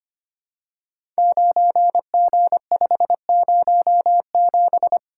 Morse code, heard as a keyed tone, about 25 wpm, 700 Hz. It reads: 9G507
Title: Code Groups - 5 Letter:Numbers - 25wpm